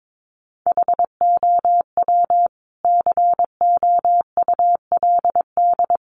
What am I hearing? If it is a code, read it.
Morse code, heard as a keyed tone, about 22 wpm, 700 Hz. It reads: HOW COULD
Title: Sentences from Top 1000 Words 22wpm